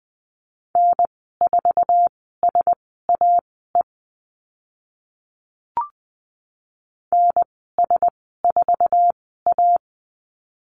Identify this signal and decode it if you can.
Morse code, heard as a keyed tone, about 20 wpm, spaced as for 14 wpm, 700 Hz. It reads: N4SAE NS4A
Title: 055 Lesson 10g - Letter r - Call Signs 20wpm